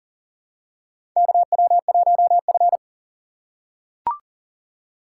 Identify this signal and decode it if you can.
Morse code, heard as a keyed tone, about 40 wpm, 700 Hz. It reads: KW1F
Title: USA Call Signs - 4 Characters - 8x Character Spacing - 40wpm